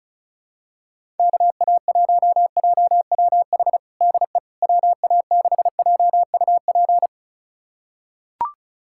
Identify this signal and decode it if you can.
Morse code, heard as a keyed tone, about 36 wpm, 700 Hz. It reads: KA1JWH DE WA6JUP
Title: USA Callsign DE Callsign - 6 Characters - 35wpm